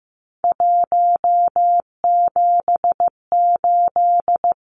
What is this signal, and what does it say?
Morse code, heard as a keyed tone, about 15 wpm, 700 Hz. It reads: 178
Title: Numbers - 3-Digit 15wpm